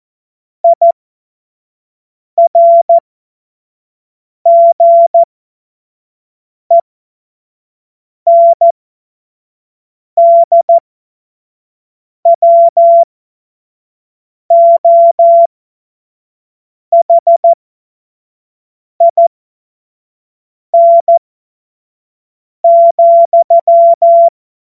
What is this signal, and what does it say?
Morse code, heard as a keyed tone, about 14 wpm, spaced as for 5 wpm, 680 Hz. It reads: IRGENDWOHIN,
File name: Morsekurs / Witze / Witz 21. März 2023 25 F70